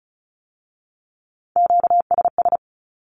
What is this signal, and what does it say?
Morse code, heard as a keyed tone, about 36 wpm, 700 Hz. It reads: QSS
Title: Q-Codes with Spoken Meaning - 35wpm